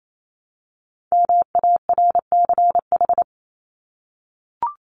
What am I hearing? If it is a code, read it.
Morse code, heard as a keyed tone, about 28 wpm, 700 Hz. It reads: MARCH
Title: Top 1501-1750 Words 28wpm